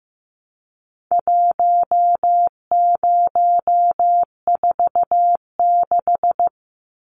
Morse code, heard as a keyed tone, about 15 wpm, 700 Hz. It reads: 1046
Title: Numbers - 4-Digit 15wpm